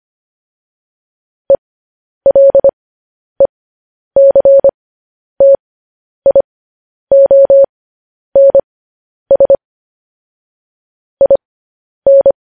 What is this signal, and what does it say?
Morse code, heard as a keyed tone, about 25 wpm, spaced as for 10 wpm, 550 Hz. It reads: ELECTIONS IN
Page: News Headlines in Morse Code at 10 WPM